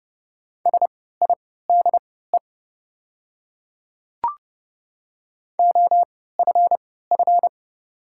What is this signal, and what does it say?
Morse code, heard as a keyed tone, about 30 wpm, spaced as for 17 wpm, 700 Hz. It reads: SIDE OFF
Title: 3x Character Spacing - Top 500 Words - 30wpm